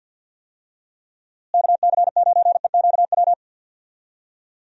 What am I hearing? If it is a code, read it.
Morse code, heard as a keyed tone, about 52 wpm, 700 Hz. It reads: KC9EQW
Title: USA Call Signs - 6 Characters - Repeated 2x - 50wpm